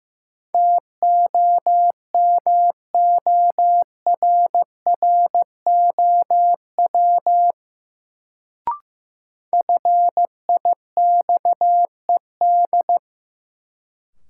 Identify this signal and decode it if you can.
Morse code, heard as a keyed tone, about 16 wpm, 700 Hz. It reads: TOMORROW FIXED
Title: Sentences from Top 1100 Words - New Word Drill Alt - 15wpm